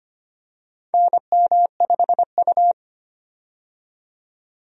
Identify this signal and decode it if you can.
Morse code, heard as a keyed tone, about 25 wpm, 700 Hz. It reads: NM5U
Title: USA Call Signs - 4 Characters - 25wpm